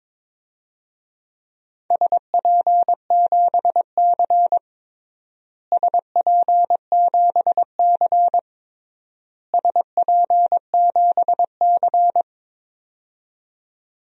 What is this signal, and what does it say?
Morse code, heard as a keyed tone, about 22 wpm, 700 Hz. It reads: SP7C SP7C SP7C
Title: DXCC-100 Call Signs - 4 Characters - Repeated 3x - 22wpm